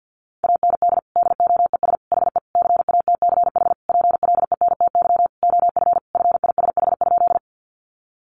Morse code, heard as a keyed tone, about 50 wpm, 700 Hz. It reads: AND DOES HE CATCH PLENTY OF FISH?
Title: Sentences from Top 1500 Words 50wpm